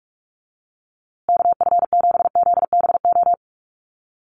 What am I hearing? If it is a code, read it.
Morse code, heard as a keyed tone, about 44 wpm, 700 Hz. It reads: KF7ZBO
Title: USA Call Signs Repeated 3x 45wpm